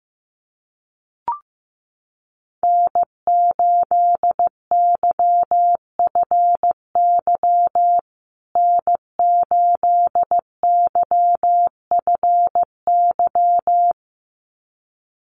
Morse code, heard as a keyed tone, about 15 wpm, 700 Hz. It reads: N8YFY N8YFY
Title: USA Call Signs Repeated 2x 15wpm